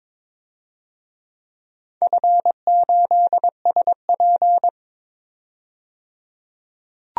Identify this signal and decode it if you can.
Morse code, heard as a keyed tone, about 22 wpm, 700 Hz. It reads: F8SP
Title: DXCC-100 Call Signs - 4 Characters - 22wpm